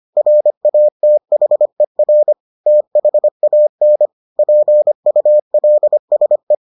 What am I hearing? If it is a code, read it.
Morse code, heard as a keyed tone, about 25 wpm, 600 Hz. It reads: RATHER THAN PULSE